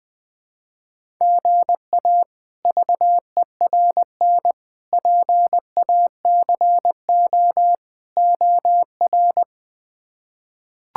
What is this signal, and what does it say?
Morse code, heard as a keyed tone, about 20 wpm, 700 Hz. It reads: GA VERN PACO OR
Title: K1USN SST QSO Practice - 20wpm